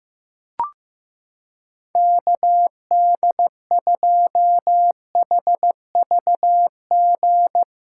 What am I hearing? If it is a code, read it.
Morse code, heard as a keyed tone, about 16 wpm, 700 Hz. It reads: KD2HVG
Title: USA Callsign DE Callsign - 6 Characters - 15wpm